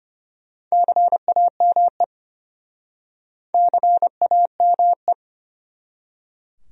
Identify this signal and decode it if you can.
Morse code, heard as a keyed tone, about 34 wpm, 700 Hz. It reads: CAME CAME